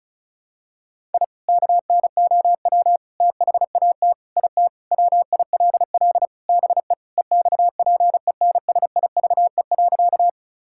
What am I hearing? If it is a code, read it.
Morse code, heard as a keyed tone, about 36 wpm, 700 Hz. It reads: I KNOW THAT IT WILL BE EXPENSIVE.
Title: Sentences from Top 1000 Words - Review Part 6 - 35wpm